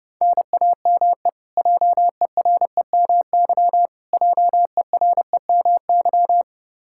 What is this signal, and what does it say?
Morse code, heard as a keyed tone, about 30 wpm, 700 Hz. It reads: NAME JEREMY JEREMY
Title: Practice QSO - Exchange 1 - Beginning - 30wpm